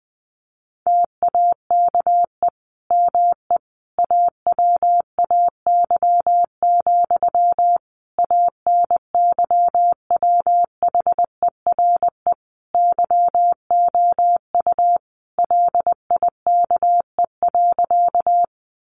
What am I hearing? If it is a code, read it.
Morse code, heard as a keyed tone, about 20 wpm, 700 Hz. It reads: TAKE ME AWAY, ANYWHERE YOU LIKE.
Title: Sentences from Top 1400 Words 20wpm